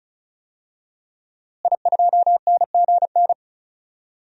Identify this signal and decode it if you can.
Morse code, heard as a keyed tone, about 36 wpm, 700 Hz. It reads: I2NGN
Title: DXCC-100 Call Signs - 5 Characters - Repeated 2x - 35wpm